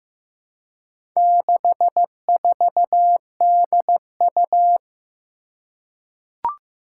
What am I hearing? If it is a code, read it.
Morse code, heard as a keyed tone, about 15 wpm, 700 Hz. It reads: 64DU
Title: Four Challenging Letter-Number Combinations 15wpm